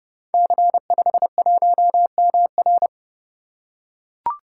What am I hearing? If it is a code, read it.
Morse code, heard as a keyed tone, about 30 wpm, 700 Hz. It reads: C51MR
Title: Code Groups - 5 Letter:Numbers - 30wpm